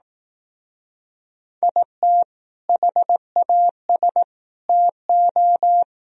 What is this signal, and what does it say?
Morse code, heard as a keyed tone, about 19 wpm, 700 Hz. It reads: IT HAS TO
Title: Speed-Racing - Sets of 4 Words 15wpm